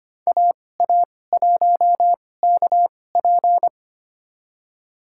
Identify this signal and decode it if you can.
Morse code, heard as a keyed tone, about 25 wpm, spaced as for 18 wpm, 700 Hz. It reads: AA1KP
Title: USA Call Signs - 5 Characters - 2x Character Spacing - 25wpm